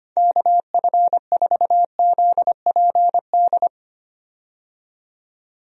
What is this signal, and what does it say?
Morse code, heard as a keyed tone, about 25 wpm, 700 Hz. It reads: KF4ZPD
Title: USA Call Signs - 6 Characters - 25wpm